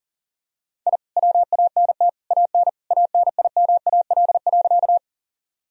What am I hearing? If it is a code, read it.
Morse code, heard as a keyed tone, about 40 wpm, 700 Hz. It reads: I WANT AN ANIMAL.
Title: Sentences from Top 1000 Words - Review Part 1 - 40wpm